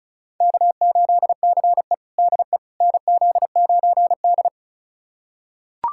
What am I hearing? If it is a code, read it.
Morse code, heard as a keyed tone, about 36 wpm, 700 Hz. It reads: K8CE DE NZ9D
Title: USA Callsign DE Callsign - 4 Characters - 35wpm